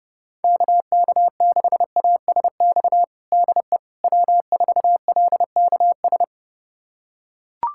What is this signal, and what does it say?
Morse code, heard as a keyed tone, about 30 wpm, 700 Hz. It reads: KK6ASX DE W4LKS
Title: USA Callsign DE Callsign 30wpm